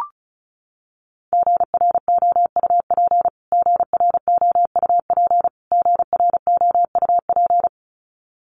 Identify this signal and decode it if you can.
Morse code, heard as a keyed tone, about 34 wpm, 700 Hz. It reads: GROUP GROUP GROUP
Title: Top 500 Words Repeated 35wpm